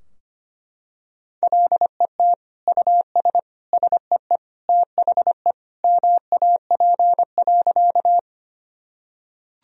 Morse code, heard as a keyed tone, about 25 wpm, 700 Hz. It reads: LET US SEE THE MAP.
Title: Sentences from Top 1000 Words - Review Part 2 - 25wpm